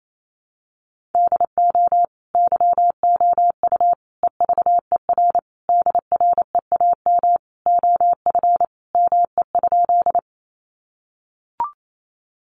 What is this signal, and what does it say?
Morse code, heard as a keyed tone, about 28 wpm, 700 Hz. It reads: DO YOU EVER DREAM OF ME?
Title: Sentences from Top 1500 Words 28wpm